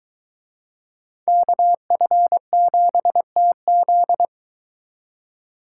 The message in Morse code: KF7TZ

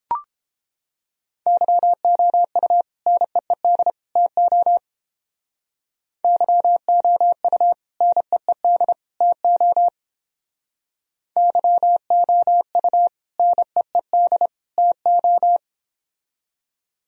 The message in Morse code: YOU NEED TO YOU NEED TO YOU NEED TO

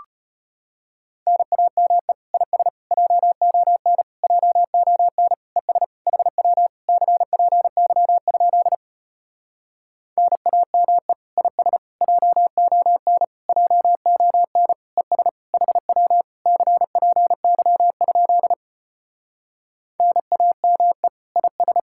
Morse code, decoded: NAME IS JON JON ES HW CPY? NAME IS JON JON ES HW CPY? NAME IS

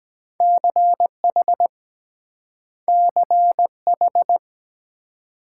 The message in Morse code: CH CH

